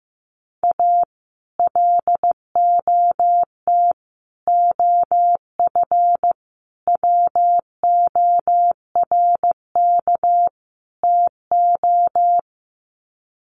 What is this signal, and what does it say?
Morse code, heard as a keyed tone, about 15 wpm, 700 Hz. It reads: A LOT OF WORK TO
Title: Sets of 5 Words Encore2 15wpm